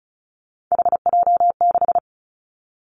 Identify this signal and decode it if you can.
Morse code, heard as a keyed tone, about 36 wpm, 700 Hz. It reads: HJ6
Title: Callsign Prefixes for Top 100 DXCC - 35wpm